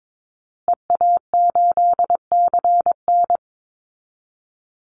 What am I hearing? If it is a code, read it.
Morse code, heard as a keyed tone, about 22 wpm, 690 Hz. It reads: EA8CN
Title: Call Signs Repeated 2x - 22wpm